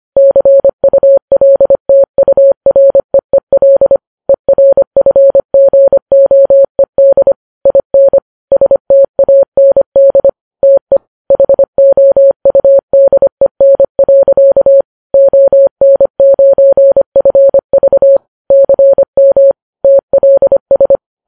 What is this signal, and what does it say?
Morse code, heard as a keyed tone, about 25 wpm, 550 Hz. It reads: CULTUREEL ERFGOED IN STAND TE HOUDEN. ON9FV CM TLS
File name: Files Web tls / Morse on9fv / morse25kps